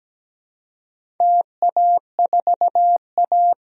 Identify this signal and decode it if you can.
Morse code, heard as a keyed tone, about 17 wpm, 700 Hz. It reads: TA4A
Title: DXCC-100 Call Signs - 4 Characters - 17wpm